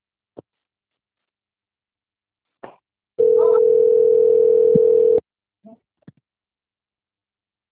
Ringback tone (North America), 440 + 480 Hz.